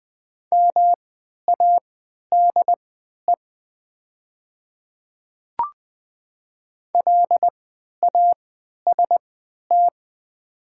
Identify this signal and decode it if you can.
Morse code, heard as a keyed tone, about 20 wpm, spaced as for 11 wpm, 700 Hz. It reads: MADE LAST
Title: 3x Character Spacing - Top 300 Words - 20wpm